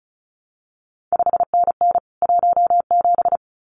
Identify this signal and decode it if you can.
Morse code, heard as a keyed tone, about 36 wpm, 700 Hz. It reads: 5NN 17
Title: CQ WW DX Contest Exchange Encore - 35wpm